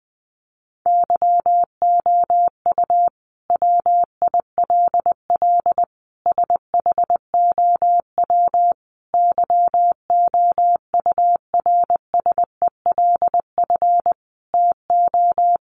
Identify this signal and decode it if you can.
Morse code, heard as a keyed tone, about 20 wpm, 700 Hz. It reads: YOU WILL SHOW YOURSELF TO
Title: Sentences from Top 600 Words 20wpm